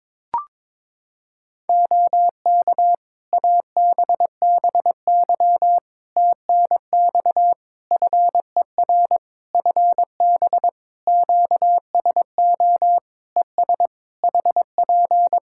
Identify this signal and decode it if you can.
Morse code, heard as a keyed tone, about 22 wpm, 700 Hz. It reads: OK ABBY TNX FER FB QSO ES HP
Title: Practice QSO - Exchange 4 - Ending - 22wpm